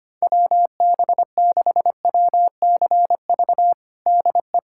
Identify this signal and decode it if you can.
Morse code, heard as a keyed tone, about 25 wpm, 700 Hz. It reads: WB6WCV DE
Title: USA Callsign DE Callsign - 6 Characters - 25wpm